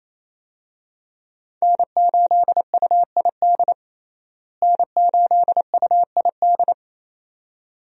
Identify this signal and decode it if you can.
Morse code, heard as a keyed tone, about 28 wpm, 700 Hz. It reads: N8UID N8UID
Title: USA Call Signs - 5 Characters - Repeated 2x - 28wpm